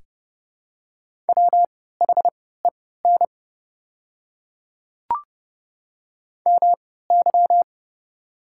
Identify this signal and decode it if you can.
Morse code, heard as a keyed tone, about 30 wpm, spaced as for 17 wpm, 690 Hz. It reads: WHEN MY
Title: 3x Character Spacing - Top 100 Words - 30wpm